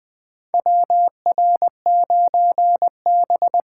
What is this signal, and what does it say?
Morse code, heard as a keyed tone, about 20 wpm, 700 Hz. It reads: WR9B
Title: USA Call Signs - 4 Characters - 4x Character Spacing - 20wpm